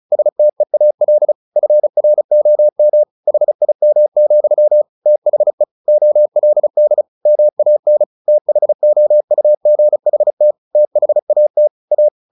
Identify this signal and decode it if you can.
Morse code, heard as a keyed tone, about 34 wpm, 600 Hz. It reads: STEAL FROM HIM, THE OLD MAN THOUGHT THAT A
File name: the old man and the sea / the old man and the sea 1 35wpm